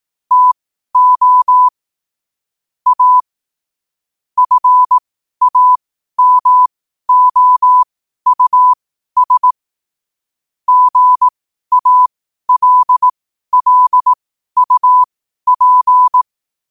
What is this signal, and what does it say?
Morse code, heard as a keyed tone, about 18 wpm, spaced as for 12 wpm, 1000 Hz. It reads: TO A FAMOUS GALLUP
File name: qotdfiles / 12wpm